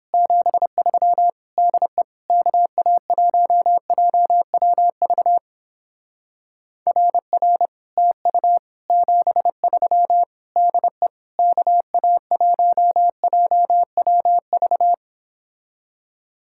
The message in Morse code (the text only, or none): 73 DE KA1JWV RR TU 73 DE KA1JWV